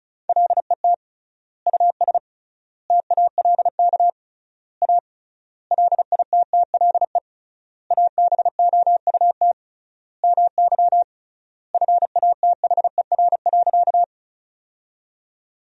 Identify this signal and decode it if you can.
Morse code, heard as a keyed tone, about 34 wpm, 700 Hz. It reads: LET US TALK A LITTLE ABOUT MY FATHER.
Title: 3x Word-Spacing - Sentences from Top 400 Words - 35wpm